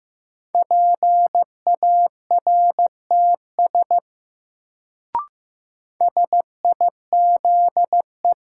Morse code, heard as a keyed tone, about 15 wpm, 700 Hz. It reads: PARTS SIZE
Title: Sets of 3 Words - New Word Drill - 15wpm